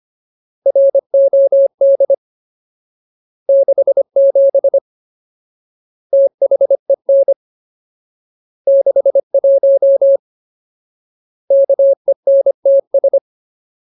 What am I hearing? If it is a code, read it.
Morse code, heard as a keyed tone, about 25 wpm, 550 Hz. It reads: ROD 67 THEN 61 KENTS
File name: Ultimate Word List / 25WPM Ultimate